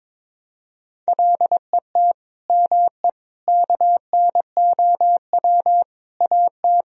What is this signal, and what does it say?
Morse code, heard as a keyed tone, about 22 wpm, 700 Hz. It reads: LET ME KNOW AT
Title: Sentences from Top 300 Words and Abbreviations 22wpm